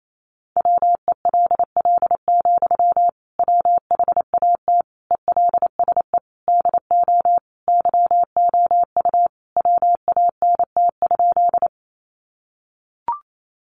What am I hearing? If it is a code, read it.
Morse code, heard as a keyed tone, about 28 wpm, 700 Hz. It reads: WELL, WHAT ELSE DO YOU WANT?
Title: Sentences from Top 800 Words 28wpm